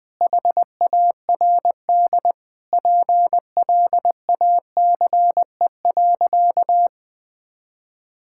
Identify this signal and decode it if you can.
Morse code, heard as a keyed tone, about 21 wpm, 700 Hz. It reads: HARD PLACE.
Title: Sentences with 7-Word Phrases - 20wpm